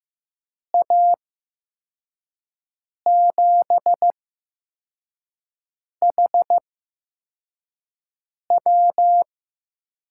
Morse code, heard as a keyed tone, about 15 wpm, spaced as for 4 wpm, 700 Hz. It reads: A7HW
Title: USA Call Signs - 5 Characters - 8x Character Spacing - 15wpm